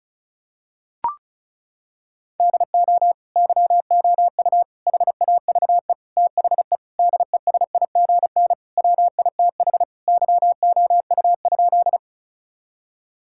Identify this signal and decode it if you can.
Morse code, heard as a keyed tone, about 36 wpm, 700 Hz. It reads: DO YOU HAVE THE DESIGN WITH YOU?
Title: Sentences from Top 1000 Words - Review Part 5 - 35wpm